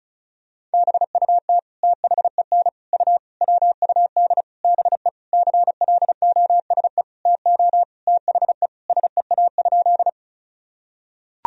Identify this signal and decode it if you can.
Morse code, heard as a keyed tone, about 34 wpm, 700 Hz. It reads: BUT THEN U WUD BE CLOSE TO THE SEA?
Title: Sentences from Top 300 Words and Abbreviations 35wpm